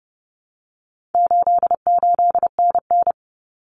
Morse code, heard as a keyed tone, about 30 wpm, 700 Hz. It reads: 88NN
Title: Common Cut Numbers - 4-Digit 30wpm